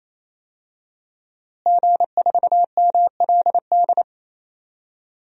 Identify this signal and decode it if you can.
Morse code, heard as a keyed tone, about 28 wpm, 700 Hz. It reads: G4MLD
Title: DXCC-100 Call Signs - 5 Characters - Repeated 3x - 28wpm